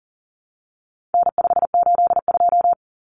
Morse code, heard as a keyed tone, about 40 wpm, 700 Hz. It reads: N582
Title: Common Cut Numbers - 4-Digit 40wpm